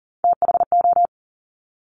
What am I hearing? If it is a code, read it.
Morse code, heard as a keyed tone, about 42 wpm, 700 Hz. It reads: THO
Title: Top Three Letter Combinations in Words 40wpm